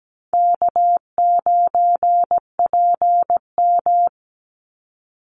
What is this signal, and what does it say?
Morse code, heard as a keyed tone, about 17 wpm, 700 Hz. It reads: K9PM
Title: NA SOTA Chaser Callsigns 17wpm